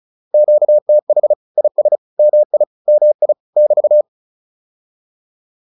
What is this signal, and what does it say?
Morse code, heard as a keyed tone, about 34 wpm, 600 Hz. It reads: QTH IS MI MI <BT>